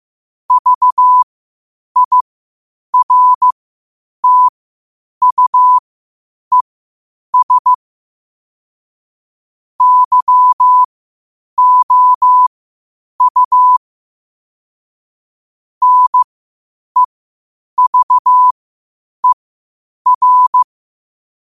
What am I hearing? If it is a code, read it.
Morse code, heard as a keyed tone, about 15 wpm, spaced as for 8 wpm, 1000 Hz. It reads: VIRTUES YOU NEVER
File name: qotdfiles / 8wpm